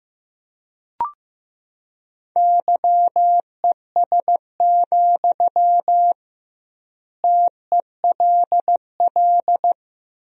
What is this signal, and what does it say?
Morse code, heard as a keyed tone, about 15 wpm, 700 Hz. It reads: YES, TELL
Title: 2x Word-Spacing - Sentences from Top 500 Words - 15wpm